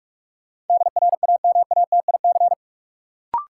Morse code, heard as a keyed tone, about 41 wpm, 700 Hz. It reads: DRAMATIC
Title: Sentences from Top 1300 Words - New Word Drill Alt - 45wpm